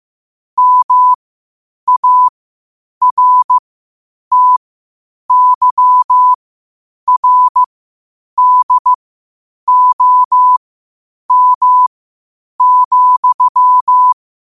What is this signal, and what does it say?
Morse code, heard as a keyed tone, about 15 wpm, spaced as for 8 wpm, 1000 Hz. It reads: MARTYRDOM,
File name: qotdfiles / 8wpm